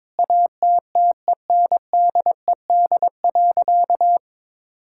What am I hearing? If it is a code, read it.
Morse code, heard as a keyed tone, about 22 wpm, 700 Hz. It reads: ATTENDED.
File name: practice / 22wpm / Sentences with 4-Word Phrases - 22wpm